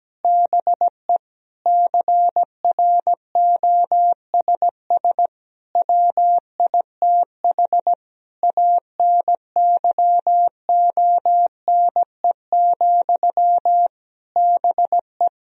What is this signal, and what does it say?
Morse code, heard as a keyed tone, about 17 wpm, 700 Hz. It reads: BE CROSS WITH ANYONE, BE
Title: Sentences from Top 1000 Words - Review Part 1 - 17wpm